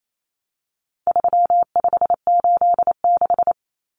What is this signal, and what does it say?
Morse code, heard as a keyed tone, about 28 wpm, 700 Hz. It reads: 3586
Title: Numbers - 4-Digit 28wpm